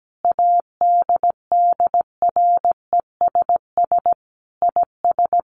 Morse code, heard as a keyed tone, about 17 wpm, 700 Hz. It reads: ADDRESS IS
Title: Sets of 5 Words Encore2 17wpm